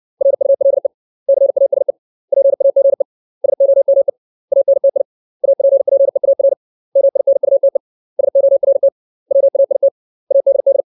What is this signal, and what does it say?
Morse code, heard as a keyed tone, about 60 wpm, 550 Hz. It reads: RULE CASE PAGE SOME ANTI APPEAR MINUTE SORT WAIT ADD
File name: 632 words / 632 Words 60WPM